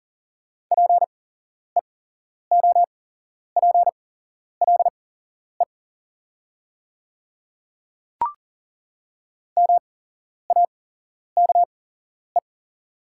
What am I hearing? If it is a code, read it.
Morse code, heard as a keyed tone, about 40 wpm, spaced as for 11 wpm, 700 Hz. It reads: PEOPLE MAKE
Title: 8x Character Spacing - Top 100 Words - 40wpm